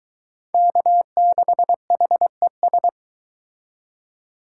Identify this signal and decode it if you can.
Morse code, heard as a keyed tone, about 22 wpm, 700 Hz. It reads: K6HES